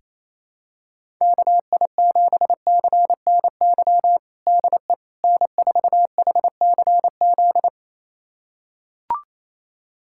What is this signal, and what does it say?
Morse code, heard as a keyed tone, about 27 wpm, 700 Hz. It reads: KI7CNY DE N4HCZ
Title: USA Callsign DE Callsign 28wpm